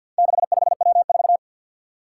Morse code, heard as a keyed tone, about 51 wpm, 700 Hz. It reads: 65W4
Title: Four Challenging Letter-Number Combinations 50wpm